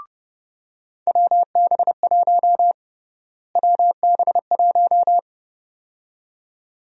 Morse code, heard as a keyed tone, about 30 wpm, 700 Hz. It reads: WB1 WB1
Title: USA Call Sign Prefixes - Repeated 2x - 30wpm